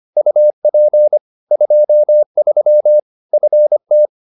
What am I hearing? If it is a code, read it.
Morse code, heard as a keyed tone, about 25 wpm, 600 Hz. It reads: UP 23 FT